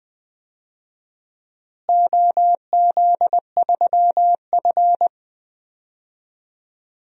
Morse code, heard as a keyed tone, about 20 wpm, 700 Hz. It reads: OZ3F
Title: DXCC-100 Call Signs - 4 Characters - Repeated 3x - 20wpm